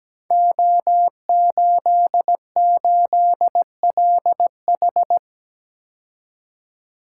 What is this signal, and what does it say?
Morse code, heard as a keyed tone, about 17 wpm, 700 Hz. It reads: O88LH
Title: Code Groups - 5 Letter:Numbers - 17wpm